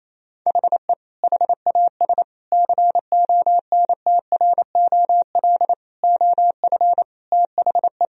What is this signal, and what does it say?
Morse code, heard as a keyed tone, about 28 wpm, 700 Hz. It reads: HE HAS CONTROL OF THE
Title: Sentences from Top 1000 Words - Review Part 3 - 28wpm